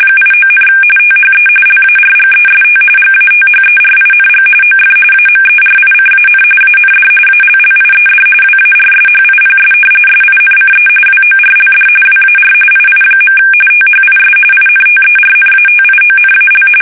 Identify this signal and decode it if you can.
RTTY, about 45.45 baud. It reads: 89, THE LARGEST GATHERING OF AMERICAN AND ALLIED WARSHIPS SINCE WORLD WAR II. THIS INTERNATIONAL
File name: fsk850 / battleshipnj